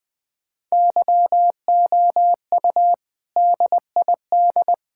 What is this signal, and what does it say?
Morse code, heard as a keyed tone, about 20 wpm, 700 Hz. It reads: YOU DID
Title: Sentences with 3-Word Phrases - 20wpm